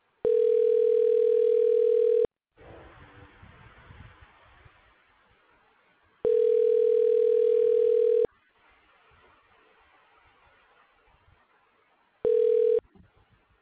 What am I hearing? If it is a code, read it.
Ringback tone (North America), 440 + 480 Hz.